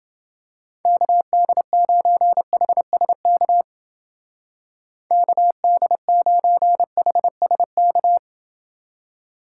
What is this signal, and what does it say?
Morse code, heard as a keyed tone, about 28 wpm, 700 Hz. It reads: KD9HSK KD9HSK